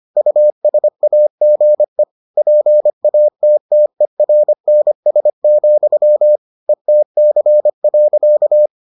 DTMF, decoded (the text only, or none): none